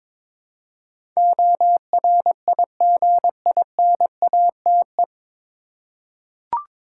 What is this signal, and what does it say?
Morse code, heard as a keyed tone, about 22 wpm, 700 Hz. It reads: ORIGINATE
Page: Morse Code Ninja practice 22wpm